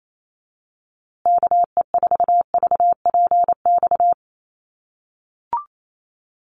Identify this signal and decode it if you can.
Morse code, heard as a keyed tone, about 28 wpm, 700 Hz. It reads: KE4VPX
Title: USA Call Signs Repeated 2x 28wpm